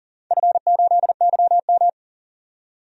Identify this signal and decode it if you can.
Morse code, heard as a keyed tone, about 38 wpm, 700 Hz. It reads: F8YM
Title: DXCC-100 Call Signs - 4 Characters - 40wpm